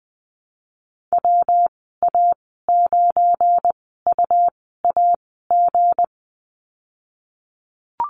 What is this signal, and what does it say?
Morse code, heard as a keyed tone, about 20 wpm, spaced as for 15 wpm, 700 Hz. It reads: WA9UAG
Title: 193 Lesson 31g - Letter v - Call Signs 20wpm